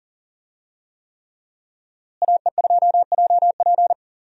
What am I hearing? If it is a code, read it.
Morse code, heard as a keyed tone, about 37 wpm, 700 Hz. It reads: AE2JP
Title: USA Call Signs - 5 Characters - Repeated 3x - 40wpm